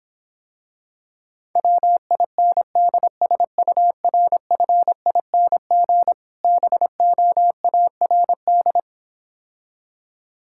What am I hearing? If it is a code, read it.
Morse code, heard as a keyed tone, about 26 wpm, 700 Hz. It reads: WINDSURFING BOARD